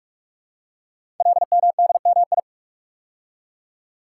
Morse code, heard as a keyed tone, about 44 wpm, 700 Hz. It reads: LMDMI
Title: Code Groups - 5 Letters - 45wpm